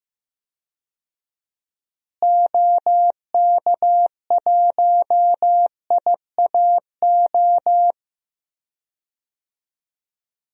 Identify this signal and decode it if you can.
Morse code, heard as a keyed tone, about 15 wpm, 700 Hz. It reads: OK1IAO
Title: DXCC-100 Call Signs - 6 Characters - Repeated 3x - 15wpm